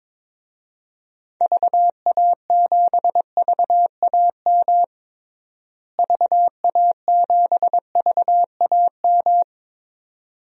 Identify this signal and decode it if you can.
Morse code, heard as a keyed tone, about 22 wpm, 700 Hz. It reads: VA7VAM VA7VAM
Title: DXCC-100 Call Signs - 6 Characters - Repeated 2x - 22wpm